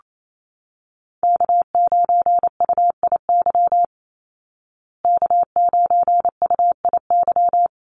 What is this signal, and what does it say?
Morse code, heard as a keyed tone, about 28 wpm, 700 Hz. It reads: K9UIY K9UIY
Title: Call Signs Repeated 2x - 28wpm